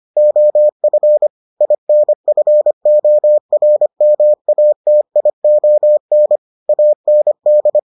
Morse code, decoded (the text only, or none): OF INFORMATION AND